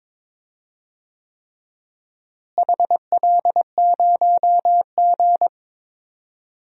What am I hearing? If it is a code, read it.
Morse code, heard as a keyed tone, about 22 wpm, 700 Hz. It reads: HL0G
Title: DXCC-100 Call Signs - 4 Characters - Repeated 3x - 22wpm